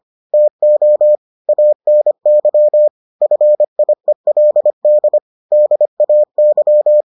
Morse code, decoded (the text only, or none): TO ANY FIELD DAY